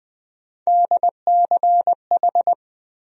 Morse code, heard as a keyed tone, about 20 wpm, 700 Hz. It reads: DCH
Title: USA Call Sign Postfixes - 3 Letters 20wpm